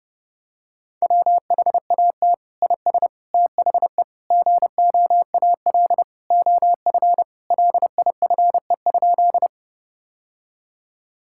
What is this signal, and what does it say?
Morse code, heard as a keyed tone, about 32 wpm, 700 Hz. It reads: WHAT IS THE GOAL OF LIFE?
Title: Sentences from Top 1000 Words - Review Part 5 - 30wpm